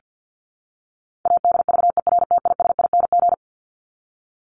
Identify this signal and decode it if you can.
Morse code, heard as a keyed tone, about 48 wpm, 700 Hz. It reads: ADVERTISING
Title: Top 751-1000 Words 50wpm